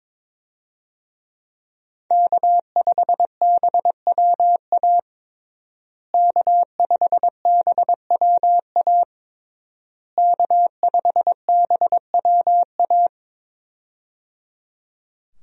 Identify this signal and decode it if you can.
Morse code, heard as a keyed tone, about 22 wpm, 700 Hz. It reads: K5BWA K5BWA K5BWA
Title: USA Call Signs - 5 Characters - Repeated 3x - 22wpm